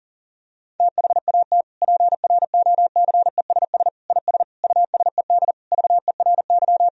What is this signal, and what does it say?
Morse code, heard as a keyed tone, about 40 wpm, 700 Hz. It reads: THAT PROCESS IS USED VERY
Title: Sentences from Top 1000 Words - Review Part 3 - 40wpm